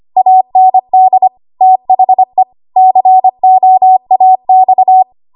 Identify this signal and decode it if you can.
Morse code, heard as a keyed tone, about 25 wpm, 750 Hz. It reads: AND THE COAX
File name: Archive / 25 WPM / 25WPM